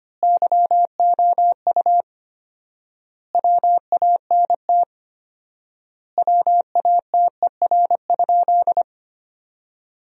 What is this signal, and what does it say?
Morse code, heard as a keyed tone, about 25 wpm, 700 Hz. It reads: YOU WANT WATER?
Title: 4x Word-Spacing - Sentences from Top 200 Words - 25wpm